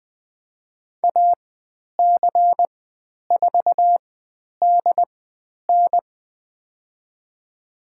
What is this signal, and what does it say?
Morse code, heard as a keyed tone, about 20 wpm, spaced as for 10 wpm, 700 Hz. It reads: AC4DN
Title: 102 Lesson 17g - Letter u - Call Signs 20wpm